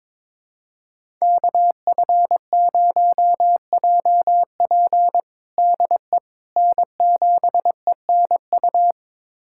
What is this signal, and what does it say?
Morse code, heard as a keyed tone, about 22 wpm, 700 Hz. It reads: KF0JP DE N7ENU
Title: USA Callsign DE Callsign - 5 Characters - 22wpm